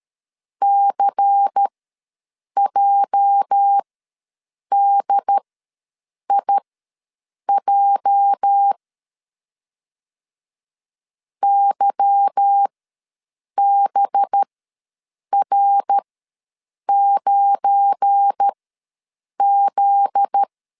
Morse code, heard as a keyed tone, about 13 wpm, spaced as for 7 wpm, 800 Hz. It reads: CJDIJ YBR9Z
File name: LEZIONI / 8 YJ45 / 8 YJ ws 12wpm 3342